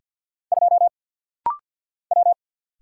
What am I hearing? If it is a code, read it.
Morse code, heard as a keyed tone, about 50 wpm, 700 Hz. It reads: 2 W
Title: Single Letter-Number - Rapid-Fire 50wpm